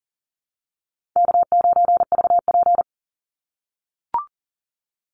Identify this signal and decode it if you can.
Morse code, heard as a keyed tone, about 40 wpm, 700 Hz. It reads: K9VP
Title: Call Signs - 40wpm